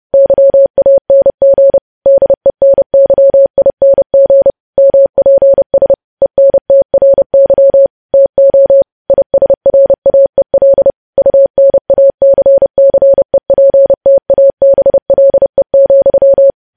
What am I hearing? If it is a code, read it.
Morse code, heard as a keyed tone, about 30 wpm, 550 Hz. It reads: YANG DENYING MPS ENTRY TO ISRAEL UNACCEPTABLE,
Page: news headlines in morse code at 30 wpm